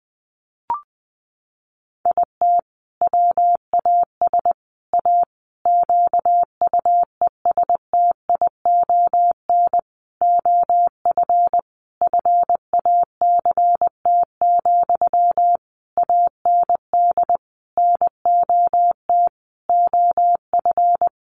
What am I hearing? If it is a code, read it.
Morse code, heard as a keyed tone, about 20 wpm, 700 Hz. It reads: IT WAS A QUESTION OF FACT, AND NOT OF
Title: Sentences from Top 1400 Words 20wpm